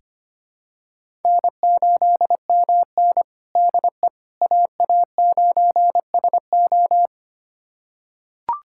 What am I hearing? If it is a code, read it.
Morse code, heard as a keyed tone, about 25 wpm, 700 Hz. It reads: N8MN DE AA9SO
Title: USA Callsign DE Callsign 25wpm